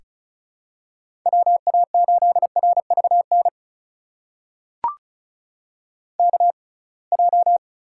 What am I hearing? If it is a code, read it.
Morse code, heard as a keyed tone, about 36 wpm, spaced as for 27 wpm, 700 Hz. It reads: WA8RVN KJ
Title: USA Call Signs - 6 Characters - 6x Character Spacing - 35wpm